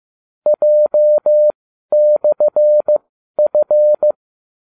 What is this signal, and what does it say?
Morse code, heard as a keyed tone, about 15 wpm, spaced as for 12 wpm, 600 Hz. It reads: J/F
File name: cw / ubacw30 3